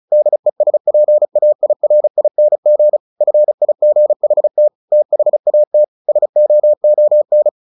Morse code, heard as a keyed tone, about 34 wpm, 600 Hz. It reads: DESPAIRING FIGHT THAT SOON
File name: the old man and the sea / the old man and the sea 1 35wpm